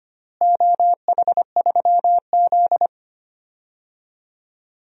Morse code, heard as a keyed tone, about 25 wpm, 700 Hz. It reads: OH3Z
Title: DXCC-100 Call Signs - 4 Characters - Repeated 2x - 25wpm